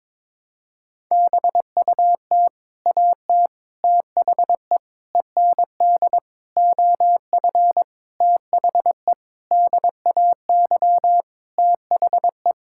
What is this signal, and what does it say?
Morse code, heard as a keyed tone, about 22 wpm, 700 Hz. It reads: BUT AT THE END OF THE DAY THE
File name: practice / 22wpm / Sentences with 8-Word Phrases - 22wpm